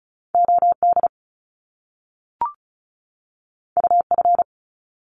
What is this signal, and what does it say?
Morse code, heard as a keyed tone, about 36 wpm, 700 Hz. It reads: OD UF
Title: Top Two Letter Combinations in Words 35wpm